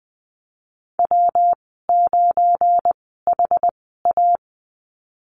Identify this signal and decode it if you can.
Morse code, heard as a keyed tone, about 20 wpm, spaced as for 15 wpm, 700 Hz. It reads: W9HA
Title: 231 Lesson 37g - Letter x - Call Signs 20wpm